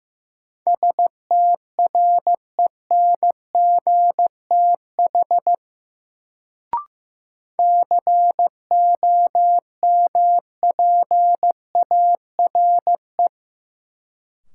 Morse code, heard as a keyed tone, about 16 wpm, 700 Hz. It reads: STRENGTH COMPARE
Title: Sentences from Top 1300 Words - New Word Drill Alt - 15wpm